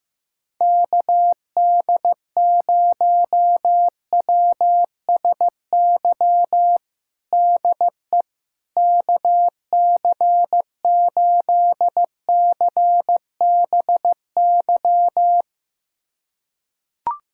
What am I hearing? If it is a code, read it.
Morse code, heard as a keyed tone, about 15 wpm, 700 Hz. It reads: KD0WSY DE KC8CBY
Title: USA Callsign DE Callsign 15wpm